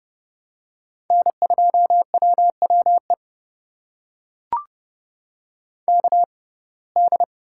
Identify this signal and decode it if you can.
Morse code, heard as a keyed tone, about 32 wpm, spaced as for 22 wpm, 700 Hz. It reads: N2WWE KD
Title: USA Call Signs - 5 Characters - 6x Character Spacing - 30wpm